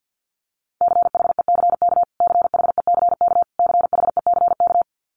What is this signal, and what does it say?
Morse code, heard as a keyed tone, about 50 wpm, 700 Hz. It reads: CHECK CHECK CHECK
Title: Top 500 Words Repeated 50wpm